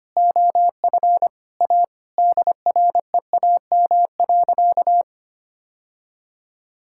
Morse code, heard as a keyed tone, about 25 wpm, 700 Hz. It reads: OF A DREAM.
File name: practice / 25wpm / Sentences with 4-Word Phrases - 25wpm